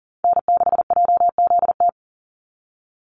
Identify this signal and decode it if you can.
Morse code, heard as a keyed tone, about 41 wpm, 700 Hz. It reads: N6JZT
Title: NA SOTA Chaser Callsigns 40wpm